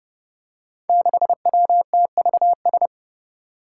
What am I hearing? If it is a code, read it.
Morse code, heard as a keyed tone, about 30 wpm, 700 Hz. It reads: 6WTVS
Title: Five Challenging Letter-Number Combinations 30wpm